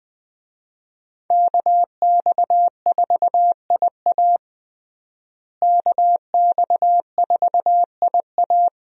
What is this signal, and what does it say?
Morse code, heard as a keyed tone, about 20 wpm, 700 Hz. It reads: KX4IA KX4IA
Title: USA Call Signs - 5 Characters - Repeated 3x - 20wpm